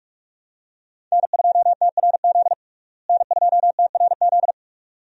Morse code, heard as a keyed tone, about 46 wpm, 700 Hz. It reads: N2TRZ N2TRZ
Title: USA Call Signs - 5 Characters - Repeated 2x - 45wpm